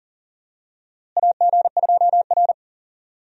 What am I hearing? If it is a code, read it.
Morse code, heard as a keyed tone, about 40 wpm, 700 Hz. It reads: AG2R
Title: USA Call Signs - 4 Characters - Repeated 3x - 40wpm